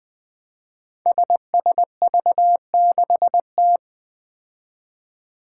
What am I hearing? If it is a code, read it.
Morse code, heard as a keyed tone, about 21 wpm, 700 Hz. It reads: SSV6T
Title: Five Challenging Letter-Number Combinations 20wpm